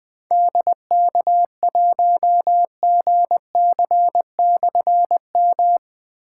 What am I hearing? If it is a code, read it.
Morse code, heard as a keyed tone, about 20 wpm, 700 Hz. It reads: DK1GC/M
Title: Difficult Call Signs 20wpm